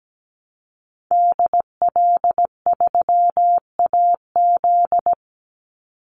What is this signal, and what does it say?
Morse code, heard as a keyed tone, about 17 wpm, 700 Hz. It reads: DL3AZ
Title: Call Signs Repeated 3x - 17wpm